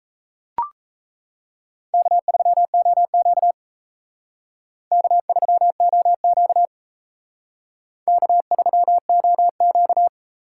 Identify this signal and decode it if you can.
Morse code, heard as a keyed tone, about 35 wpm, 700 Hz. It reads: K3OQ K3OQ K3OQ
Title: Speed Racing - USA 1x2 Call Signs 28wpm